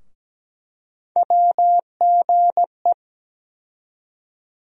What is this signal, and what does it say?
Morse code, heard as a keyed tone, about 17 wpm, 700 Hz. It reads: WGE
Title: USA Call Sign Postfixes - 3 Letters 17wpm